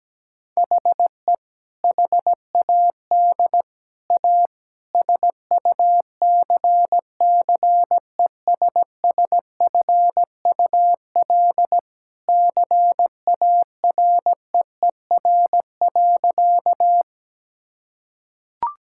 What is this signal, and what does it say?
Morse code, heard as a keyed tone, about 17 wpm, 700 Hz. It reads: HE HAD A SUCCESSFUL CAREER.
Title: Sentences from Top 1000 Words - Review Part 6 - 17wpm